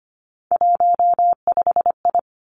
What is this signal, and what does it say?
Morse code, heard as a keyed tone, about 25 wpm, 700 Hz. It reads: 15I
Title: Three Challenging Letter-Number Combinations 25wpm